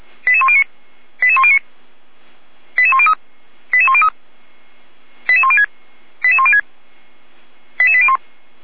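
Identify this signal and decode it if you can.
Five-tone selective call (ZVEI), heard as quick runs of five tones, tones of about 80 ms. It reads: 88109 88109 88103 88103 88106 88106 88801
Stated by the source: ZVEI five-tone call received via SDR.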